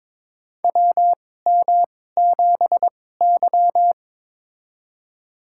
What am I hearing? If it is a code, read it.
Morse code, heard as a keyed tone, about 21 wpm, spaced as for 16 wpm, 700 Hz. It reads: WM7Y
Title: USA Call Signs - 4 Characters - 2x Character Spacing - 22wpm